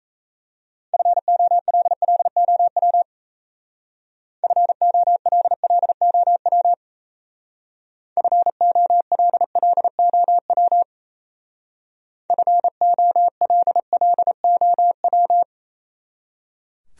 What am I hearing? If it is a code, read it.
Morse code, heard as a keyed tone, about 35 wpm, 700 Hz. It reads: FOLLOW FOLLOW FOLLOW FOLLOW